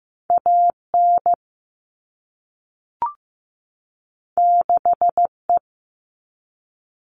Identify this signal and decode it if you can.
Morse code, heard as a keyed tone, about 15 wpm, 700 Hz. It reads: AN 6E
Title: All Cut Numbers - 2-Digit 15wpm